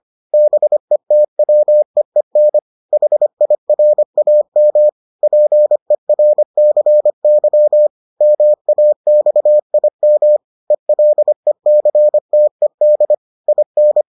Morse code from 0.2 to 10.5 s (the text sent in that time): BETWEEN HIRAM PERCY MAXIM